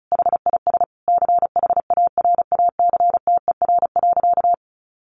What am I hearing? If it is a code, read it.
Morse code, heard as a keyed tone, about 34 wpm, 700 Hz. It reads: HIS CHARACTER.
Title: Sentences from Top 1200 Words 35wpm